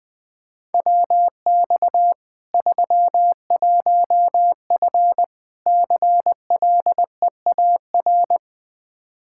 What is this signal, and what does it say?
Morse code, heard as a keyed tone, about 21 wpm, 700 Hz. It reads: WX 31F CLEAR
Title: Practice QSO - Exchange 2 - Info - 20wpm